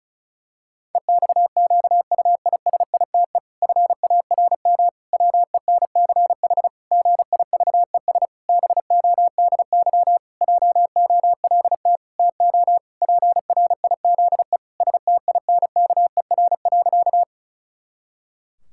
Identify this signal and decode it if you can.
Morse code, heard as a keyed tone, about 34 wpm, 700 Hz. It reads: EXQUISITE FARM WENCH GIVES BODY JOLT TO PRIZE STINKER.
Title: Pangram Challenge v2 - 35wpm